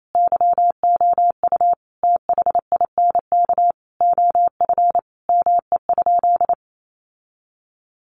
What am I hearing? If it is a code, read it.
Morse code, heard as a keyed tone, about 28 wpm, 700 Hz. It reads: YOU THINK OF ME?
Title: Sentences from Top 1100 Words 28wpm